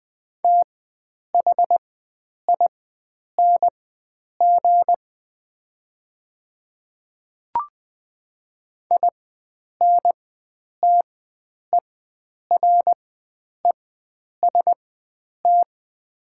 Morse code, heard as a keyed tone, about 20 wpm, spaced as for 9 wpm, 690 Hz. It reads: THING INTEREST
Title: 4x Character Spacing - Top 500 Words - 20wpm